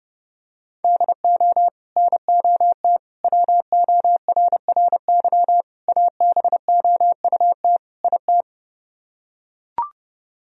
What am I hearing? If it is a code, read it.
Morse code, heard as a keyed tone, about 32 wpm, 700 Hz. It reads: DO NOT WORRY ABOUT IT